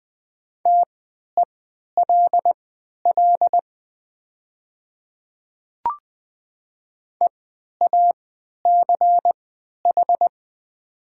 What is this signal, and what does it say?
Morse code, heard as a keyed tone, about 20 wpm, spaced as for 11 wpm, 700 Hz. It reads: TELL EACH
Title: 3x Character Spacing - Top 200 Words - 20wpm